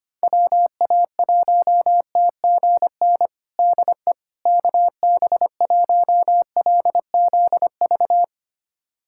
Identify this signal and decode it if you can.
Morse code, heard as a keyed tone, about 25 wpm, 700 Hz. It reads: WA1TGN DE KB1LZV
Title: USA Callsign DE Callsign - 6 Characters - 25wpm